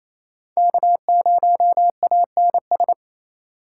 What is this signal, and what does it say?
Morse code, heard as a keyed tone, about 28 wpm, 690 Hz. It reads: K0ANS
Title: USA Call Signs - 5 Characters - 4x Character Spacing - 28wpm